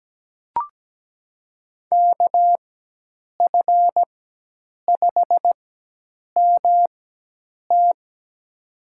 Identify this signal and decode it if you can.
Morse code, heard as a keyed tone, about 17 wpm, spaced as for 8 wpm, 700 Hz. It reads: KF5MT
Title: USA Call Signs - 5 Characters - 4x Character Spacing - 17wpm